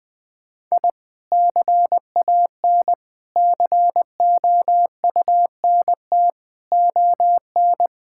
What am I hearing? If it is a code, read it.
Morse code, heard as a keyed tone, about 21 wpm, 700 Hz. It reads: I CAN COUNT ON
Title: Sentences with 9-Word Phrases - Spoken Beginning - 20wpm